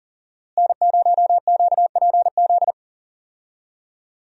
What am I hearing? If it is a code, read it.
Morse code, heard as a keyed tone, about 40 wpm, 700 Hz. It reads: N0QPZ
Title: USA Call Signs - 5 Characters - Repeated 3x - 40wpm